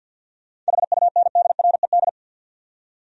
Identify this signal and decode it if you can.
Morse code, heard as a keyed tone, about 62 wpm, 700 Hz. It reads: HUNDRED
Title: Speed-Racing Top 500 Words 50wpm